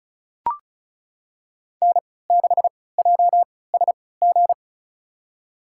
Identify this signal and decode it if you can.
Morse code, heard as a keyed tone, about 36 wpm, spaced as for 20 wpm, 700 Hz. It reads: N6JSG
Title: USA Call Signs - 5 Characters - 3x Character Spacing - 35wpm